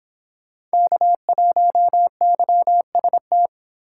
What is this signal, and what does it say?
Morse code, heard as a keyed tone, about 26 wpm, 700 Hz. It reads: K1YST